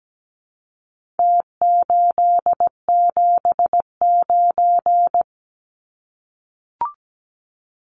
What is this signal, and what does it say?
Morse code, heard as a keyed tone, about 17 wpm, 700 Hz. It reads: T879
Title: Numbers - Contest Style - 4-Digit 17wpm